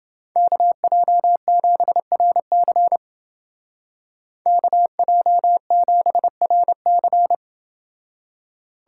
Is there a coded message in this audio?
Yes: Morse code from 0.4 to 7.3 s, heard as a keyed tone, about 29 wpm, 700 Hz. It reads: KJ7RC KJ7RC